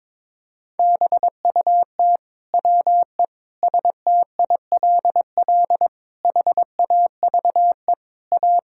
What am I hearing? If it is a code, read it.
Morse code, heard as a keyed tone, about 22 wpm, 700 Hz. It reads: BUT WE STILL HAVE A
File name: Sets of 9 Words - 22wpm